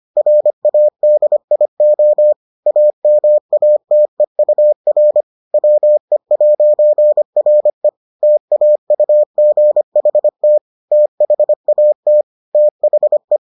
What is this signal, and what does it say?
Morse code, heard as a keyed tone, about 25 wpm, 600 Hz. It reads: RADIO AMATEUR WE'RE TAUGHT THAT THE